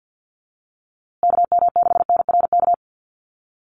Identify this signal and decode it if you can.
Morse code, heard as a keyed tone, about 50 wpm, 700 Hz. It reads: KM6NRK
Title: USA Call Signs Repeated 2x 50wpm